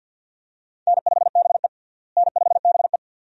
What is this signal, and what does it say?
Morse code, heard as a keyed tone, about 51 wpm, 700 Hz. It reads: N5BE N5BE
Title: USA Call Signs - 4 Characters - Repeated 2x - 50wpm